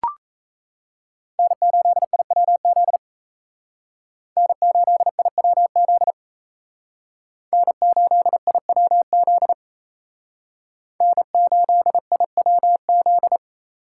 Morse code, heard as a keyed tone, about 35 wpm, 700 Hz. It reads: N8IWZ N8IWZ N8IWZ N8IWZ